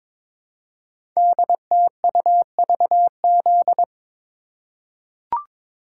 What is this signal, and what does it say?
Morse code, heard as a keyed tone, about 22 wpm, 700 Hz. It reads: DTUVZ
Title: Code Groups - 5 Letters - 22wpm